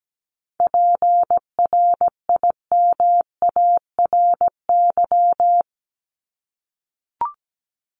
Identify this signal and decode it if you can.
Morse code, heard as a keyed tone, about 17 wpm, 700 Hz. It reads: PRIMARY
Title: LY Suffix - Top 100 Words with Base Words 17wpm